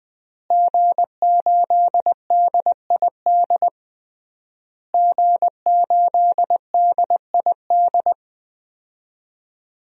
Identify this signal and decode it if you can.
Morse code, heard as a keyed tone, about 20 wpm, 700 Hz. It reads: G8DID G8DID
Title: DXCC-100 Call Signs - 5 Characters - Repeated 3x - 20wpm